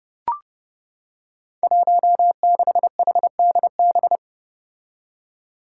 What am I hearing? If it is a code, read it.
Morse code, heard as a keyed tone, about 30 wpm, 700 Hz. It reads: 16HDB
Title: Five Challenging Letter-Number Combinations 30wpm